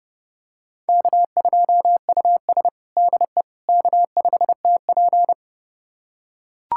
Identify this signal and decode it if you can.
Morse code, heard as a keyed tone, about 30 wpm, 700 Hz. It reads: K2US DE K5TP
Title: USA Callsign DE Callsign - 4 Characters - 30wpm